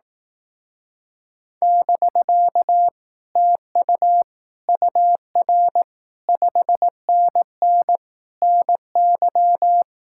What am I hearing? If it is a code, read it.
Morse code, heard as a keyed tone, about 18 wpm, 700 Hz. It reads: <BK> TU UR 5NN NY
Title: Speed-Racing Common QSO Phrases Encore - 15wpm